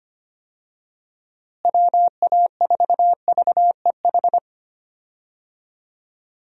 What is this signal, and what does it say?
Morse code, heard as a keyed tone, about 25 wpm, 700 Hz. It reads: WA4VEH
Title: USA Call Signs - 6 Characters - Repeated 2x - 25wpm